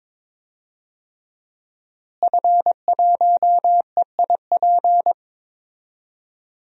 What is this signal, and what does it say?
Morse code, heard as a keyed tone, about 22 wpm, 700 Hz. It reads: F1EIP
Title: DXCC-100 Call Signs - 5 Characters - Repeated 3x - 22wpm